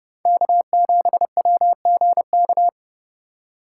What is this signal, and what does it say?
Morse code, heard as a keyed tone, about 29 wpm, 700 Hz. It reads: K7WGK
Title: USA Call Signs - 5 Characters - 3x Character Spacing - 30wpm